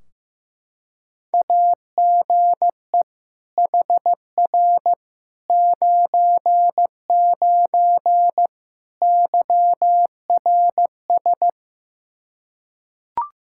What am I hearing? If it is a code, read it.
Morse code, heard as a keyed tone, about 15 wpm, 700 Hz. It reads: AGE HR 99 YRS
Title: Practice QSO - Exchange 3 - More Info - 15wpm